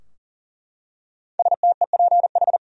Morse code, heard as a keyed tone, about 40 wpm, 700 Hz. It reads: STEPH
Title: USA Common Names - 40wpm